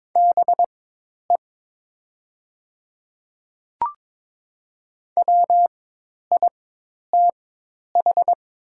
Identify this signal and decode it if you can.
Morse code, heard as a keyed tone, about 22 wpm, spaced as for 10 wpm, 700 Hz. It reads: BE WITH
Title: 4x Character Spacing - Top 200 Words - 22wpm